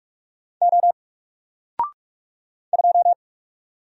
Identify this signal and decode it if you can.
Morse code, heard as a keyed tone, about 44 wpm, 700 Hz. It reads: O 2
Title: Single Character (Letter:Number:Punctuation:Prosign) 45wpm